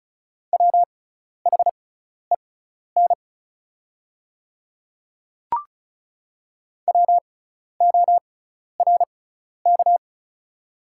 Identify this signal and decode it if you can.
Morse code, heard as a keyed tone, about 36 wpm, spaced as for 12 wpm, 690 Hz. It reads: WHEN WORK
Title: 6x Character Spacing - Top 300 Words - 35wpm